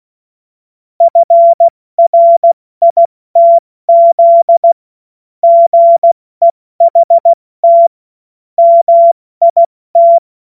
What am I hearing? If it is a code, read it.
Morse code, heard as a keyed tone, about 16 wpm, spaced as for 14 wpm, 680 Hz. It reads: FRITZ GEHT MIT
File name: Morsekurs / Witze / Witz 17. August 2021, 70 F80